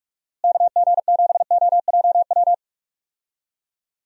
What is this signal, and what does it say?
Morse code, heard as a keyed tone, about 46 wpm, 700 Hz. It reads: KG7OJW
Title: USA Call Signs - 6 Characters - Repeated 2x - 45wpm